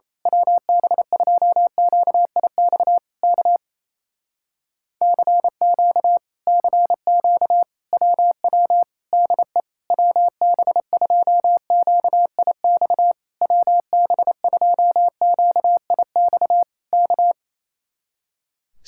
Morse code, heard as a keyed tone, about 26 wpm, 700 Hz. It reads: WB2QIX K CQ CQ WW DE WB2QIX WB2QIX K
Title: Speed-Racing Common QSO Phrases Encore - 28wpm